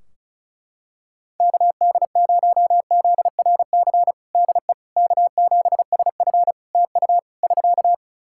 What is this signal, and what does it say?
Morse code, heard as a keyed tone, about 34 wpm, 700 Hz. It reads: KD0ZRC DE K7SF TU <SK>
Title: Practice QSO - Exchange 4 - Ending - 35wpm